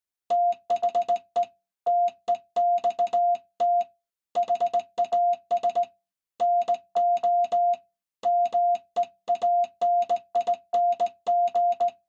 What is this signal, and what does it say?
Morse code, heard as a keyed tone, about 19 wpm, 700 Hz. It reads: THE TEXT HAS NO MEANING
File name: se / pre2 / Morse Code01-3(Fast-Key Noise)